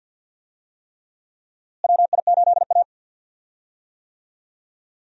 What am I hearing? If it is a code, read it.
Morse code, heard as a keyed tone, about 45 wpm, 700 Hz. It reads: WI8A
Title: USA Call Signs - 4 Characters - Repeated 3x - 50wpm